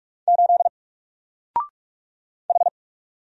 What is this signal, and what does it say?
Morse code, heard as a keyed tone, about 44 wpm, 700 Hz. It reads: 8 H
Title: Single Character (Letter:Number:Punctuation:Prosign) 45wpm